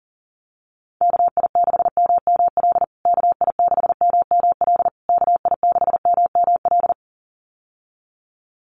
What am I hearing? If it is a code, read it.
Morse code, heard as a keyed tone, about 40 wpm, 700 Hz. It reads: KI6MML KI6MML KI6MML
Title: USA Call Signs Repeated 3x 40wpm